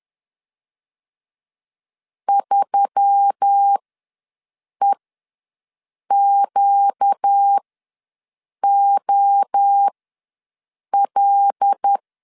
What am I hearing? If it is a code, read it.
Morse code, heard as a keyed tone, about 10 wpm, spaced as for 6 wpm, 800 Hz. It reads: 3EQOL
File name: LEZIONI / 5 LBVC / 5 L ws 10wpm 3636